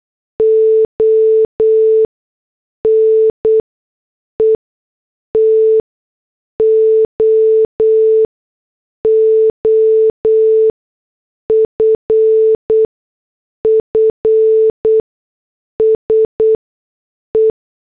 Morse code, heard as a keyed tone, about 8 wpm, spaced as for 6 wpm, 440 Hz. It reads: ONE TO OFFSE
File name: morse / headlines